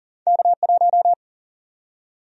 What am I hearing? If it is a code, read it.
Morse code, heard as a keyed tone, about 40 wpm, 700 Hz. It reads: K1
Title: USA Call Sign Prefixes - 40wpm